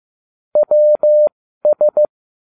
Morse code, heard as a keyed tone, about 15 wpm, spaced as for 12 wpm, 600 Hz. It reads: WS